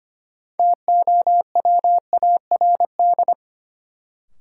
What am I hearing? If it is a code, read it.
Morse code, heard as a keyed tone, about 25 wpm, 700 Hz. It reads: TOWARD
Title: Sentences from Top 1000 Words - Review Part 6 - New Word Drill - 25wpm